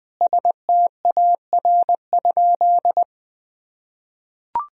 Morse code, heard as a keyed tone, about 20 wpm, 700 Hz. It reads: STAR?
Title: Sentences from Top 1000 Words - Review Part 2 - 20wpm